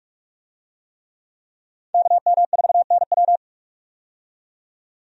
Morse code, heard as a keyed tone, about 43 wpm, 700 Hz. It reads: KM4NW
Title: USA Call Signs - 5 Characters - 45wpm